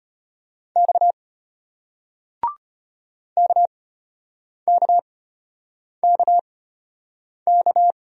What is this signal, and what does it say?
Morse code, heard as a keyed tone, about 34 wpm, spaced as for 8 wpm, 700 Hz. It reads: X KKKK